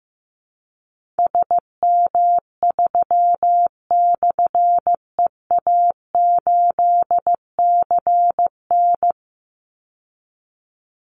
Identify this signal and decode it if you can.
Morse code, heard as a keyed tone, about 15 wpm, 700 Hz. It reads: SM3/EA8CN
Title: Call Signs Repeated 2x - 15wpm